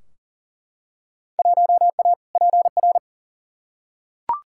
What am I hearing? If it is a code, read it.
Morse code, heard as a keyed tone, about 40 wpm, 700 Hz. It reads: 1A PR
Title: Field Day Exchanges Encore - 40wpm